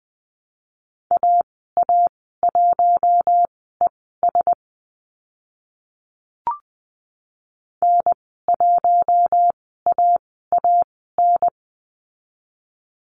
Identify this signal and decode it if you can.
Morse code, heard as a keyed tone, about 20 wpm, spaced as for 14 wpm, 700 Hz. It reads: AA1ES N1AAN
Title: 048 Lesson 9g - Number 4 - Call Signs 20wpm